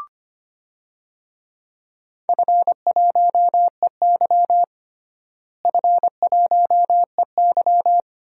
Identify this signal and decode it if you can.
Morse code, heard as a keyed tone, about 25 wpm, 700 Hz. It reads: F1EY F1EY
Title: DXCC-100 Call Signs - 4 Characters - Repeated 3x - 25wpm